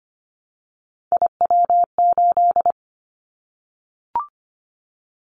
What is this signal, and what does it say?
Morse code, heard as a keyed tone, about 25 wpm, 700 Hz. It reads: IW8
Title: Callsign Prefixes for Top 25 DXCC - Repeated 3x - 25wpm